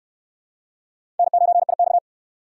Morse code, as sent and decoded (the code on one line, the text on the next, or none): -. ----. .. -..-
N9IX